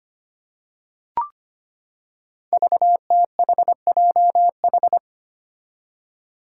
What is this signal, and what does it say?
Morse code, heard as a keyed tone, about 25 wpm, 700 Hz. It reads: VTHJH
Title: Five Challenging Letter-Number Combinations 25wpm